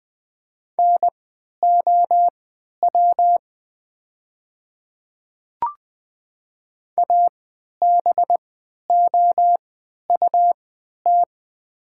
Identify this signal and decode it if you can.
Morse code, heard as a keyed tone, about 20 wpm, spaced as for 11 wpm, 700 Hz. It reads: NOW ABOUT
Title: 3x Character Spacing - Top 100 Words - 20wpm